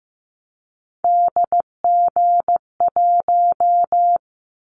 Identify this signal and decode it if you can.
Morse code, heard as a keyed tone, about 15 wpm, 700 Hz. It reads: DG1
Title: Callsign Prefixes for Top 25 DXCC - 15wpm